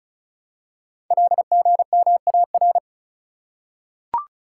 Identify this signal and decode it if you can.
Morse code, heard as a keyed tone, about 36 wpm, 700 Hz. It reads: LGMAR
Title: Code Groups - 5 Letters - 35wpm